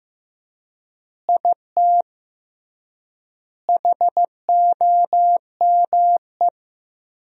Morse code, heard as a keyed tone, about 16 wpm, 700 Hz. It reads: IT HOME
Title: 3x Word-Spacing - Sentences from Top 500 Words - 15wpm